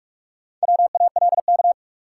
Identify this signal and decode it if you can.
Morse code, heard as a keyed tone, about 46 wpm, 700 Hz. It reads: WALK
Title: Sentences from Top 300 Words - New Word Drill - 45wpm